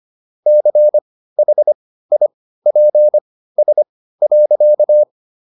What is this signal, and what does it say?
Morse code, heard as a keyed tone, about 25 wpm, spaced as for 15 wpm, 600 Hz. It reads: CHIPS.